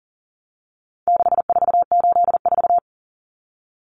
Morse code, heard as a keyed tone, about 40 wpm, 700 Hz. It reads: 6484
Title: Numbers - 4-Digit 40wpm